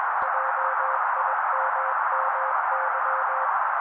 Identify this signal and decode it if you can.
Morse code, heard as a keyed tone, about 21 wpm, 530 Hz. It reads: JIMMY